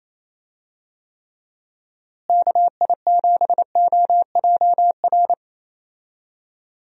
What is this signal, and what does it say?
Morse code, heard as a keyed tone, about 28 wpm, 700 Hz. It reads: KI7OJR
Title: USA Call Signs - 6 Characters - Repeated 2x - 28wpm